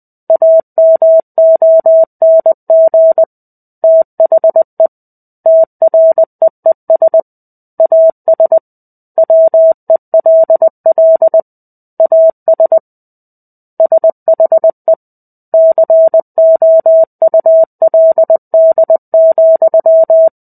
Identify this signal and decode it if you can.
Morse code, heard as a keyed tone, about 20 wpm, 650 Hz. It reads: AMONG THE TREES AS WELL AS SHE COULD,
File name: 05-Chapter-05 20 WPM